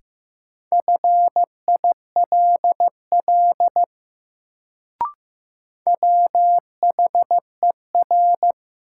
Morse code, heard as a keyed tone, about 15 wpm, 700 Hz. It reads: FILL WHER
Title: Sentences from Top 1500 Words - New Word Drill Alt - 15wpm